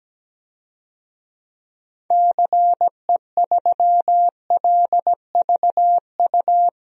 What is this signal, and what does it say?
Morse code, heard as a keyed tone, about 17 wpm, 700 Hz. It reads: CE3LVU
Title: DXCC-100 Call Signs - 6 Characters - Repeated 2x - 17wpm